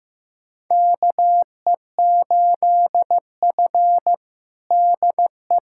Morse code, heard as a keyed tone, about 15 wpm, 700 Hz. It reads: KE8F DE
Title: USA Callsign DE Callsign - 4 Characters - 15wpm